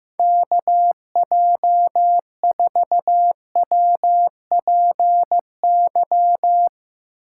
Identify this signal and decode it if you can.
Morse code, heard as a keyed tone, about 15 wpm, 700 Hz. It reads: KJ4WPY
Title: USA Call Signs - 6 Characters - 3x Character Spacing - 15wpm